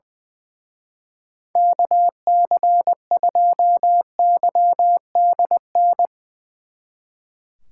Morse code, heard as a keyed tone, about 21 wpm, 700 Hz. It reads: KC2YDN
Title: Speed Racing - USA 2x3 Call Signs 20wpm